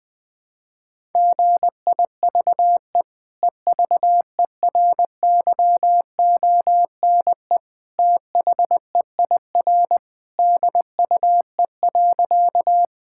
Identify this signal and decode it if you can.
Morse code, heard as a keyed tone, about 20 wpm, 700 Hz. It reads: GIVE EVERYONE THEIR DUE.
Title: Sentences from Top 1000 Words - Review Part 6 - 20wpm